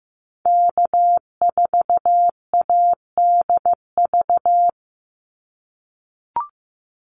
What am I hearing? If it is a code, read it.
Morse code, heard as a keyed tone, about 15 wpm, 700 Hz. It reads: K4ADV
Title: NA SOTA Activator Callsigns 15wpm